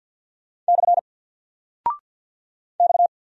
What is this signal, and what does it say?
Morse code, heard as a keyed tone, about 50 wpm, 700 Hz. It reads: / X
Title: Single Character (Letter:Number:Punctuation:Prosign) 50wpm